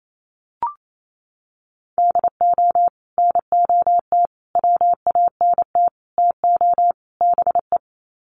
Morse code, heard as a keyed tone, about 28 wpm, 700 Hz. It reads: DO NOT WANT TO BE
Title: Sets of 5 Words 28wpm